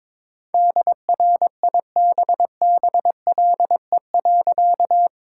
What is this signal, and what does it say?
Morse code, heard as a keyed tone, about 22 wpm, 700 Hz. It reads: DRIBBLE.
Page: morse-code-ninja-practice-22wpm